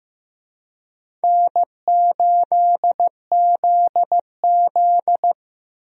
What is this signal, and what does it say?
Morse code, heard as a keyed tone, about 15 wpm, 700 Hz. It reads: N8ZZ
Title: USA Call Signs - 4 Characters - Repeated 2x - 15wpm